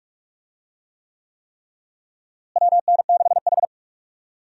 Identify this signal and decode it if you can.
Morse code, heard as a keyed tone, about 46 wpm, 700 Hz. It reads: WN6H
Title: USA Call Signs - 4 Characters - Repeated 3x - 45wpm